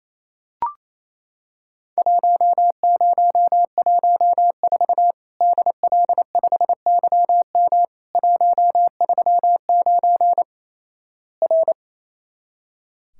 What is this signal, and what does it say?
Morse code, heard as a keyed tone, about 28 wpm, 700 Hz. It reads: 1014 DL5YM 139 R
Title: Worked All Europe DX Contest - QTC Exchange - 28wpm